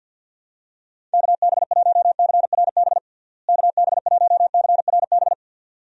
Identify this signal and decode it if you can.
Morse code, heard as a keyed tone, about 52 wpm, 700 Hz. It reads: KB1CRB KB1CRB
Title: USA Call Signs - 6 Characters - Repeated 2x - 50wpm